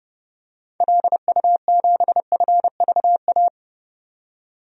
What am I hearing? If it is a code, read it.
Morse code, heard as a keyed tone, about 29 wpm, 700 Hz. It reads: LU7FVA
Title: DXCC-100 Call Signs - 6 Characters - 30wpm